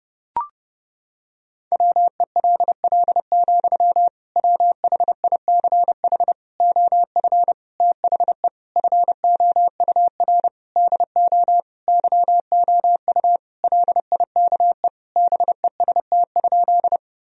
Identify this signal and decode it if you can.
Morse code, heard as a keyed tone, about 32 wpm, 700 Hz. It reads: WELL, WHICH OF THE FOUR DO YOU LIKE BEST?
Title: Sentences from Top 1000 Words - Review Part 1 - 30wpm